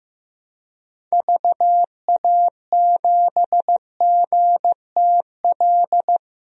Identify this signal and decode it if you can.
Morse code, heard as a keyed tone, about 15 wpm, 700 Hz. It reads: VA7GTL
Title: DXCC-100 Call Signs - 6 Characters - Repeated 3x - 15wpm